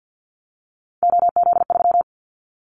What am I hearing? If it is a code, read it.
Morse code, heard as a keyed tone, about 50 wpm, 700 Hz. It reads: OZ3
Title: Callsign Prefixes for Top 50 DXCC - 50wpm